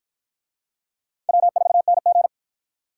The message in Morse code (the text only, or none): W4NG